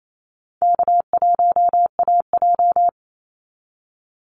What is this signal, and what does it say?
Morse code, heard as a keyed tone, about 28 wpm, 700 Hz. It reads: K1AJ
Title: Call Signs Repeated 3x - 28wpm